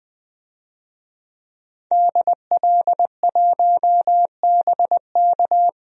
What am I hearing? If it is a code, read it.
Morse code, heard as a keyed tone, about 20 wpm, 700 Hz. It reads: DL1BK
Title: DXCC-100 Call Signs - 5 Characters - 20wpm